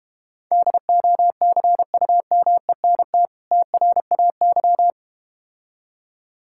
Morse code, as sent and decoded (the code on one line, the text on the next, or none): -.. --- -.-. ..- -- . -. - / - .-. .- -.--
DOCUMENT TRAY